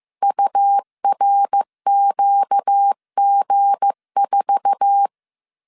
Morse code, heard as a keyed tone, about 15 wpm, 800 Hz. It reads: URQG4
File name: CW / DET / 1 9 1500C S C 14wpm 3245